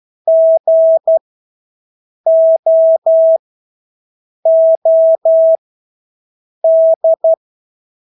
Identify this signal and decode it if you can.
Morse code, heard as a keyed tone, about 12 wpm, spaced as for 6 wpm, 650 Hz. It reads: GOOD
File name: Engelske ord 12-6 wpm